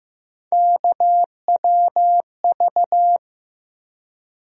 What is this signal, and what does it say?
Morse code, heard as a keyed tone, about 15 wpm, 700 Hz. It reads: KWV
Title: USA Call Sign Postfixes - 3 Letters 15wpm